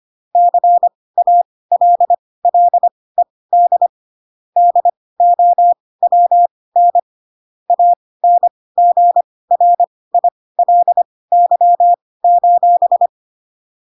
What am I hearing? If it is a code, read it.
Morse code, heard as a keyed tone, about 25 wpm, spaced as for 18 wpm, 700 Hz. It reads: CALLED DOWN ANGRILY:
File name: Pinocchio / Pinocchio ch6 12 25 ik1hge